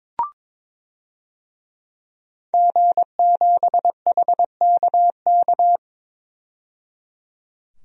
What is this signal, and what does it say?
Morse code, heard as a keyed tone, about 22 wpm, 700 Hz. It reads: G7HKK
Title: DXCC-100 Call Signs - 5 Characters - 22wpm